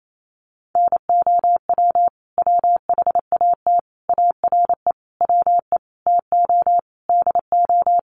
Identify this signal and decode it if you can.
Morse code, heard as a keyed tone, about 28 wpm, 700 Hz. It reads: NOW WHAT ARE WE TO DO
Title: Sentences from Top 100 Words 28wpm